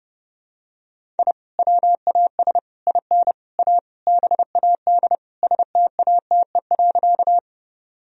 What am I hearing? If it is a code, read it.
Morse code, heard as a keyed tone, about 30 wpm, 700 Hz. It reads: I WAS IN A BAD STATE.
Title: Sentences from Top 1000 Words - Review Part 2 - 30wpm